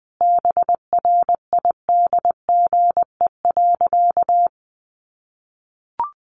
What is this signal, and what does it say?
Morse code, heard as a keyed tone, about 20 wpm, 700 Hz. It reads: BRIDGE.
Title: Sentences from Top 1200 Words 20wpm